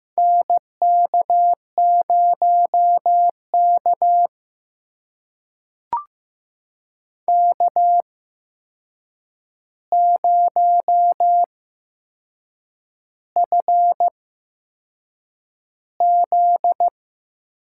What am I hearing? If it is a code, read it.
Morse code, heard as a keyed tone, about 15 wpm, spaced as for 6 wpm, 700 Hz. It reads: NK0K K0FZ
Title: USA Call Signs - 8x Character Spacing - 15wpm